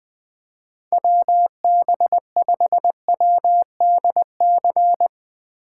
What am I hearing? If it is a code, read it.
Morse code, heard as a keyed tone, about 20 wpm, 700 Hz. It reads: WB5WDC
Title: USA Call Signs - 6 Characters - 4x Character Spacing - 20wpm